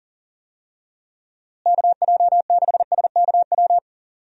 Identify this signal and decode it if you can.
Morse code, heard as a keyed tone, about 40 wpm, 700 Hz. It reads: KJ6SKW
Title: USA Call Signs - 6 Characters - Repeated 3x - 40wpm